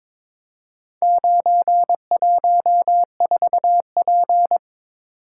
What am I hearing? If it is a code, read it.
Morse code, heard as a keyed tone, about 23 wpm, 700 Hz. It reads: 914P
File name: practice / 22wpm / Four Letter-Number Combinations - 22wpm